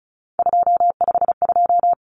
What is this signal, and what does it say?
Morse code, heard as a keyed tone, about 34 wpm, 700 Hz. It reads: 252
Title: Numbers - 3-Digit 35wpm